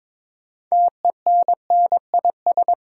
Morse code, heard as a keyed tone, about 22 wpm, 700 Hz. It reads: TENNIS
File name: Difficult Words with Hints - 22wpm